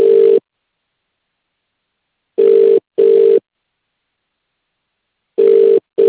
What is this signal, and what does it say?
Ringback tone (UK), 400 + 450 Hz.